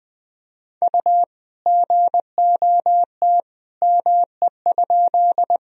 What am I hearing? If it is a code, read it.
Morse code, heard as a keyed tone, about 21 wpm, 700 Hz. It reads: U GOT ME?
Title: Sentences from Top 300 Words and Abbreviations 20wpm